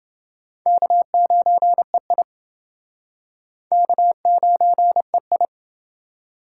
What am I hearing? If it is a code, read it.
Morse code, heard as a keyed tone, about 28 wpm, 700 Hz. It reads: K9EI K9EI